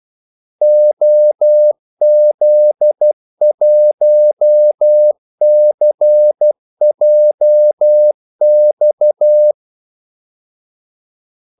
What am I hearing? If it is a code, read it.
Morse code, heard as a keyed tone, about 12 wpm, 600 Hz. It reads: OZ1CJX
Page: 2 ekstra ordmellemrum mellem kaldesignalerne